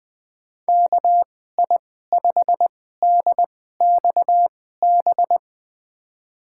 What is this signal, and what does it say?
Morse code, heard as a keyed tone, about 21 wpm, spaced as for 15 wpm, 700 Hz. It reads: KI5DXB
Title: USA Call Signs - 2x Character Spacing - 20wpm